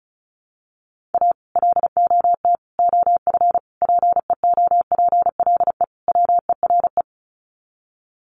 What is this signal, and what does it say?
Morse code, heard as a keyed tone, about 34 wpm, 700 Hz. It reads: A LOT OF PEOPLE WERE
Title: Sets of 5 Words Encore2 35wpm